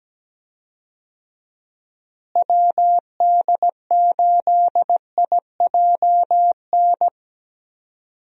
Morse code, heard as a keyed tone, about 17 wpm, 700 Hz. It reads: WD8IJN
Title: USA Call Signs - 6 Characters - Repeated 2x - 17wpm